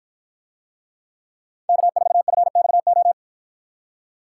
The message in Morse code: K4FXO

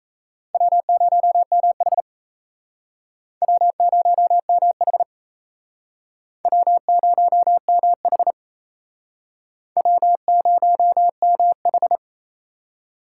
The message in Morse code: W0MH W0MH W0MH W0MH

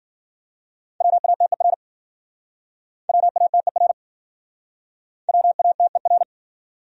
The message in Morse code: WATER WATER WATER

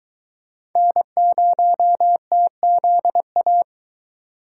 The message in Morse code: N0TZA

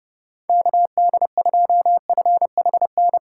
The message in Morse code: KD2FHN

